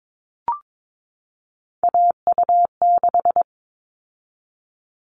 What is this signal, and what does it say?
Morse code, heard as a keyed tone, about 22 wpm, 700 Hz. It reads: AU6
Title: All Cut Numbers - 3-Digit 22wpm